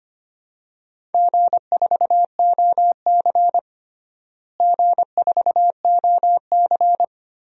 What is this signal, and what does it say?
Morse code, heard as a keyed tone, about 25 wpm, 700 Hz. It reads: G4OC G4OC
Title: DXCC-100 Call Signs - 4 Characters - Repeated 3x - 25wpm